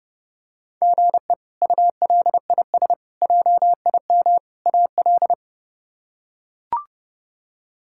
Morse code, heard as a keyed tone, about 30 wpm, 700 Hz. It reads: GE ULIS JIM AL
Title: K1USN SST QSO Practice - 30wpm